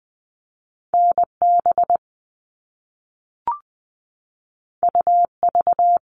Morse code, heard as a keyed tone, about 20 wpm, 700 Hz. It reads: NB UV
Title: All Cut Numbers - 2-Digit 20wpm